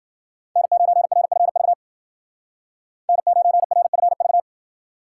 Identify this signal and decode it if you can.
Morse code, heard as a keyed tone, about 52 wpm, 700 Hz. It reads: N9RFV N9RFV